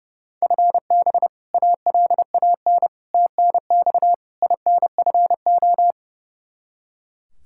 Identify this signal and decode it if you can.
Morse code, heard as a keyed tone, about 29 wpm, 700 Hz. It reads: FB ALAN TNX INFO
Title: Practice QSO - Exchange 3 - More Info - 30wpm